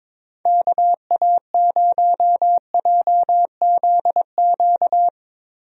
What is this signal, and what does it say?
Morse code, heard as a keyed tone, about 22 wpm, 700 Hz. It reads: KA0JZQ
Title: USA Call Signs - 6 Characters - Repeated 3x - 22wpm